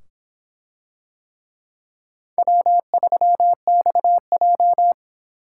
Morse code, heard as a keyed tone, about 24 wpm, 700 Hz. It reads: W3XJ